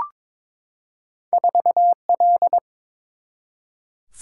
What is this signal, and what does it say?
Morse code, heard as a keyed tone, about 22 wpm, 700 Hz. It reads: 4L
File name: practice / 22wpm / Two Letter-Number Combinations - 22wpm